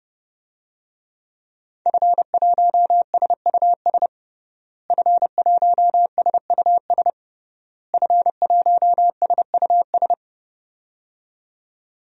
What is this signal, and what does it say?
Morse code, heard as a keyed tone, about 30 wpm, 700 Hz. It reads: F1SUS F1SUS F1SUS
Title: DXCC-100 Call Signs - 5 Characters - Repeated 3x - 30wpm